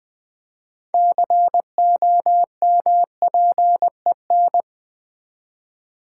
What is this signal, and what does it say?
Morse code, heard as a keyed tone, about 20 wpm, 700 Hz. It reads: COMPEN
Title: Build A Word - 6 Letters At A Time - 12-Letter Words - 20wpm